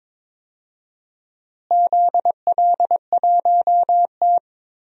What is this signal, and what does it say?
Morse code, heard as a keyed tone, about 22 wpm, 700 Hz. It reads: ZL1T
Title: DXCC-100 Call Signs - 4 Characters - Repeated 2x - 22wpm